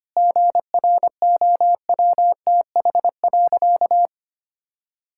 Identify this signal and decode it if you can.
Morse code, heard as a keyed tone, about 25 wpm, 700 Hz. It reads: GROWTH.
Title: Sentences from Top 1000 Words - Review Part 5 - 25wpm